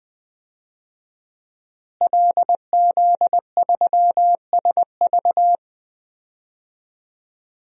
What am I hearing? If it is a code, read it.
Morse code, heard as a keyed tone, about 20 wpm, 700 Hz. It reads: LZ3SV
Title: DXCC-100 Call Signs - 5 Characters - Repeated 2x - 20wpm